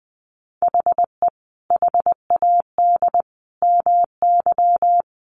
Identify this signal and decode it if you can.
Morse code, heard as a keyed tone, about 20 wpm, 700 Hz. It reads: HE HAD MY
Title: Sentences from Top 400 Words 20wpm